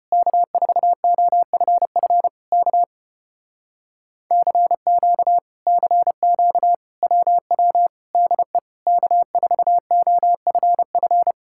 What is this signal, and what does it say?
Morse code, heard as a keyed tone, about 33 wpm, 700 Hz. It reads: K4OFF K CQ CQ WW DE K4OFF